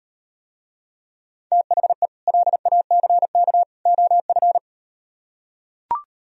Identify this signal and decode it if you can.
Morse code, heard as a keyed tone, about 38 wpm, 700 Hz. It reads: THE LACK OF